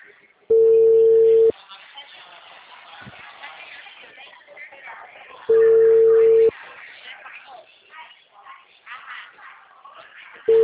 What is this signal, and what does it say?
Ringback tone (China, 450 Hz), with speech.